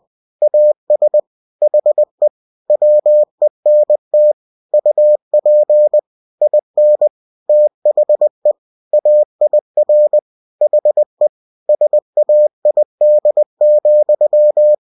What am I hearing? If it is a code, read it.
Morse code, heard as a keyed tone, about 20 wpm, 600 Hz. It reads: AS HE WENT UP IN THE AIR HE SAID,
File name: cw-books / winnie-the-pooh / winnie-20wpm-page-0070